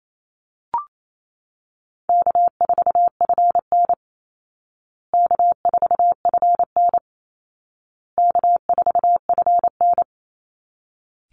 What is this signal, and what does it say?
Morse code, heard as a keyed tone, about 28 wpm, 700 Hz. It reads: K4FN K4FN K4FN
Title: Call Signs Repeated 2x - 28wpm